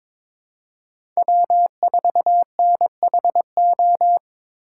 Morse code, heard as a keyed tone, about 22 wpm, 700 Hz. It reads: W4NHO
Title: USA Call Signs - 5 Characters - 3x Character Spacing - 22wpm